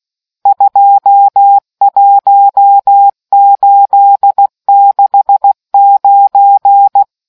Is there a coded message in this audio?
Yes: Morse code from 0.5 to 7.0 s, heard as a keyed tone, about 16 wpm, 800 Hz. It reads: 21869